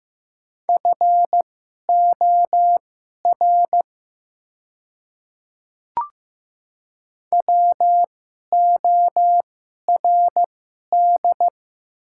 Morse code, heard as a keyed tone, about 15 wpm, spaced as for 11 wpm, 700 Hz. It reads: FOR WORD
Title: 2x Character Spacing - Top 100 Words - 15wpm